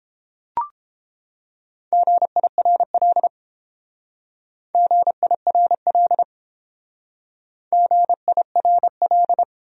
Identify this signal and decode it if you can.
Morse code, heard as a keyed tone, about 30 wpm, 700 Hz. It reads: GIRL GIRL GIRL